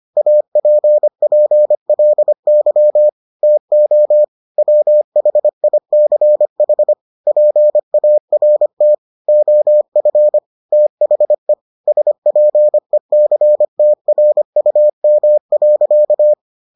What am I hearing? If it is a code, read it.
Morse code, heard as a keyed tone, about 25 wpm, 600 Hz. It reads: APPLY TO WHICH PART OF THE SPECTRUM.